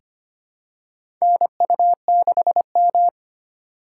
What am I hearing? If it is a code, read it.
Morse code, heard as a keyed tone, about 25 wpm, 700 Hz. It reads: NU6M
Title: USA Call Signs - 4 Characters - 2x Character Spacing - 25wpm